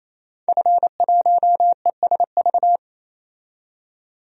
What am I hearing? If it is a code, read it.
Morse code, heard as a keyed tone, about 28 wpm, 700 Hz. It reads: F1ESV
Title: DXCC-100 Call Signs - 5 Characters - 28wpm